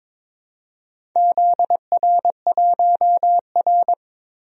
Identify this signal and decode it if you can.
Morse code, heard as a keyed tone, about 22 wpm, 700 Hz. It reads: ZR1R
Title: DXCC-100 Call Signs - 4 Characters - Repeated 3x - 22wpm